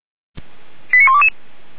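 Five-tone selective call (ZVEI), heard as quick runs of five tones, tones of about 70 ms. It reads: 89122
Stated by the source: ZVEI five-tone selective call 89123 received on an SDR.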